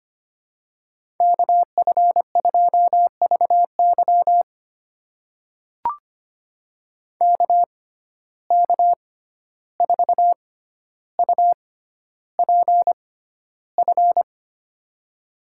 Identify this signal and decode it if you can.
Morse code, heard as a keyed tone, about 25 wpm, spaced as for 12 wpm, 700 Hz. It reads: KF2VY KK4UPF
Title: USA Call Signs - 6x Character Spacing - 25wpm